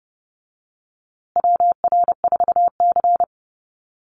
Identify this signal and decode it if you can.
Morse code, heard as a keyed tone, about 30 wpm, 700 Hz. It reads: WR4C
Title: USA Call Signs Repeated 3x 30wpm